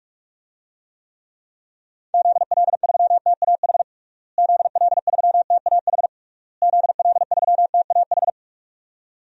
Morse code, heard as a keyed tone, about 43 wpm, 700 Hz. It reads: ZL3TAH ZL3TAH ZL3TAH
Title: DXCC-100 Call Signs - 6 Characters - Repeated 3x - 45wpm